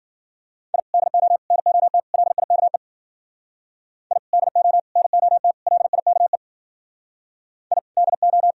Morse code, heard as a keyed tone, about 51 wpm, 700 Hz. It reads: I DO NOT LIKE I DO NOT LIKE I DO